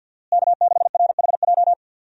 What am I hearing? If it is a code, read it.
Morse code, heard as a keyed tone, about 53 wpm, 700 Hz. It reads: K6RHJ
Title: USA Call Signs - 5 Characters - Repeated 2x - 50wpm